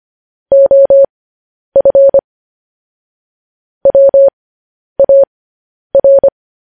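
Morse code, heard as a keyed tone, about 25 wpm, spaced as for 10 wpm, 550 Hz. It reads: OF WAR
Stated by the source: Morse transmission of news headlines.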